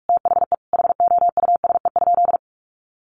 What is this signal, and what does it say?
Morse code, heard as a keyed tone, about 44 wpm, 700 Hz. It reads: THE HOUSE?
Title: Sentences from Top 800 Words 45wpm